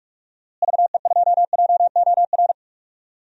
Morse code, heard as a keyed tone, about 43 wpm, 700 Hz. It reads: VE2JOR
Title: DXCC-100 Call Signs - 6 Characters - Repeated 2x - 45wpm